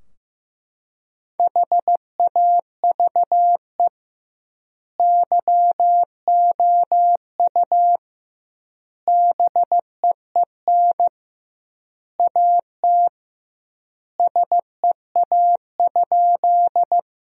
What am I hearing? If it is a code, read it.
Morse code, heard as a keyed tone, about 15 wpm, 700 Hz. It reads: HAVE YOU BEEN AT SEA?
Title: 2x Word-Spacing - Sentences from Top 300 Words - 15wpm